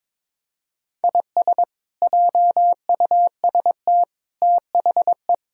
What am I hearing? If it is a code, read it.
Morse code, heard as a keyed tone, about 22 wpm, 700 Hz. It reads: IS JUST THE
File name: practice / 22wpm / Sets of 8 Words - 22wpm